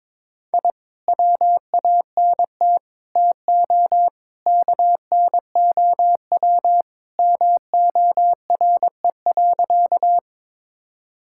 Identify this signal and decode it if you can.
Morse code, heard as a keyed tone, about 23 wpm, 700 Hz. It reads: I WANT TO KNOW MORE.
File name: practice / 22wpm / Sentences with 2-Word Phrases - 22wpm